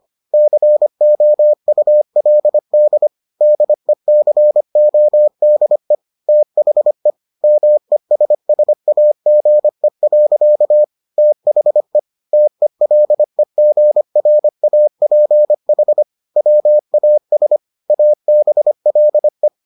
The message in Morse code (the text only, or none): COULD DECODE THE MESSAGE. THE TELEGRAPH WAS ABLE